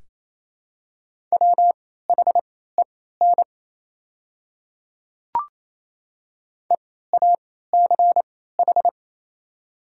Morse code, heard as a keyed tone, about 28 wpm, spaced as for 16 wpm, 700 Hz. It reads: WHEN EACH
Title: 3x Character Spacing - Top 200 Words - 28wpm